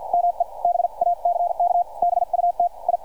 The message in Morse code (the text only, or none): ME BACK BUT I